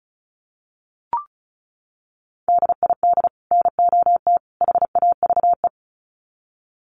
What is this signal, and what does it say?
Morse code, heard as a keyed tone, about 34 wpm, 700 Hz. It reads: DID NOT HAVE
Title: Sets of 3 Words 35wpm